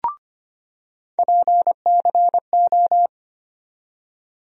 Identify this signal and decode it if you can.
Morse code, heard as a keyed tone, about 25 wpm, 700 Hz. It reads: PCO
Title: USA Call Sign Postfixes - 3 Letters 25wpm